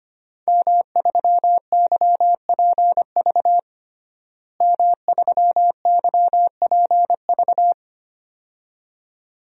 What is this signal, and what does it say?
Morse code, heard as a keyed tone, about 25 wpm, 700 Hz. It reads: M3YPV M3YPV
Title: DXCC-100 Call Signs - 5 Characters - Repeated 2x - 25wpm